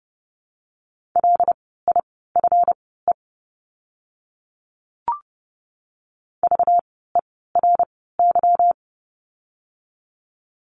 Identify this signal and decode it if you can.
Morse code, heard as a keyed tone, about 30 wpm, spaced as for 17 wpm, 700 Hz. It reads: LIFE VERY
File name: course / 30:10wpm / 237 Lesson 38f - Letter q - Words 30wpm